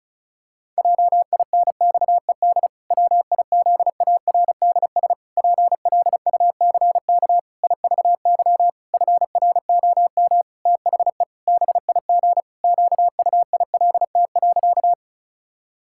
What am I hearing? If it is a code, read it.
Morse code, heard as a keyed tone, about 34 wpm, 700 Hz. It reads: JINXED WIZARDS PLUCK IVY FROM THE BIG QUILT.
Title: Pangram Challenge v2 - 35wpm